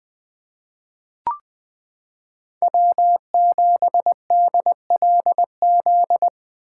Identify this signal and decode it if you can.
Morse code, heard as a keyed tone, about 21 wpm, 700 Hz. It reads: W7DLZ
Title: 2023 POTA Activator Call Signs - 20wpm